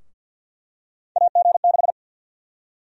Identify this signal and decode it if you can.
Morse code, heard as a keyed tone, about 52 wpm, 700 Hz. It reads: AG6
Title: USA Call Sign Prefixes - 50wpm